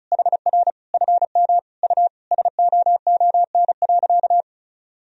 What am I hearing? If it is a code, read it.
Morse code, heard as a keyed tone, about 36 wpm, 700 Hz. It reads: HR FM U SOON.
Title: Sentences from Top 400 Words and Abbreviations 35wpm